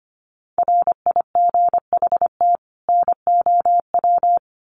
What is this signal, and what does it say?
Morse code, heard as a keyed tone, about 25 wpm, 700 Hz. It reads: RIGHT NOW
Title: Sets of 2 Words 25wpm